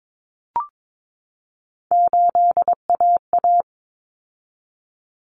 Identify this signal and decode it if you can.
Morse code, heard as a keyed tone, about 22 wpm, 700 Hz. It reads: 8AA
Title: Common Cut Numbers - 3-Digit 22wpm